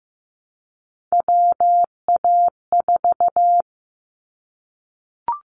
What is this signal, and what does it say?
Morse code, heard as a keyed tone, about 15 wpm, 700 Hz. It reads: WA4
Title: Callsign Prefixes for Top 25 DXCC - 15wpm